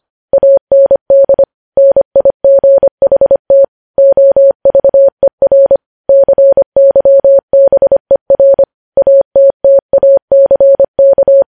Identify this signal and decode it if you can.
Morse code, heard as a keyed tone, about 25 wpm, 550 Hz. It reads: AND NIGHT OVER CYBER ATTACK
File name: podcast / News headlines-1x-25x25